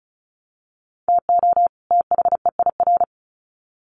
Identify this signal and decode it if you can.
Morse code, heard as a keyed tone, about 34 wpm, 700 Hz. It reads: TO THEIR
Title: Sets of 2 Words 35wpm